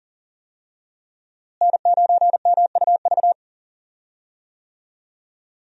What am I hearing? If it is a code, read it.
Morse code, heard as a keyed tone, about 40 wpm, 700 Hz. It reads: N9MUV
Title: USA Call Signs - 5 Characters - Repeated 2x - 40wpm